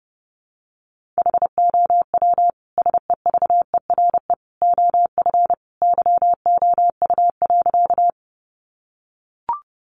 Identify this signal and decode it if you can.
Morse code, heard as a keyed tone, about 30 wpm, 700 Hz. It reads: HOW SEVERE OF YOU.
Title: Sentences from Top 1400 Words 30wpm